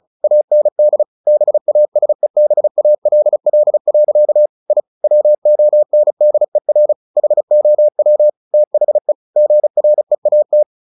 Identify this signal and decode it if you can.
Morse code, heard as a keyed tone, about 34 wpm, 600 Hz. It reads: AND BASEBALL. I WONDER HOW THE GREAT
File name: the old man and the sea 2 35wpm